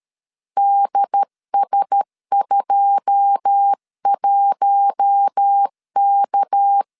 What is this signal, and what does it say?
Morse code, heard as a keyed tone, about 13 wpm, 800 Hz. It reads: DS21K
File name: LEZIONI / 9 28 12wpm 2351